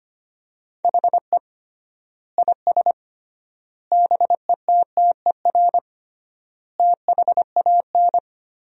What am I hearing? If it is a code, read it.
Morse code, heard as a keyed tone, about 25 wpm, 700 Hz. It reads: HE IS BETTER THAN
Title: 3x Word-Spacing - Sentences from Top 400 Words - 25wpm